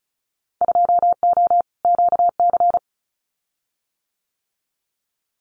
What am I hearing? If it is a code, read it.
Morse code, heard as a keyed tone, about 34 wpm, 700 Hz. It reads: 2O QC
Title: Winter Field Day Exchanges 35wpm